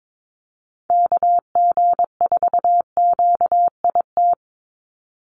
Morse code, heard as a keyed tone, about 22 wpm, 700 Hz. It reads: KG4QIT
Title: USA Call Signs 22wpm